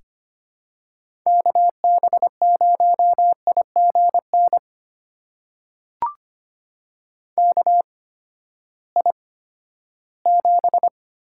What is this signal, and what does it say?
Morse code, heard as a keyed tone, about 25 wpm, spaced as for 14 wpm, 700 Hz. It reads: KB0IGN KI7
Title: USA Call Signs - 6 Characters - 8x Character Spacing - 25wpm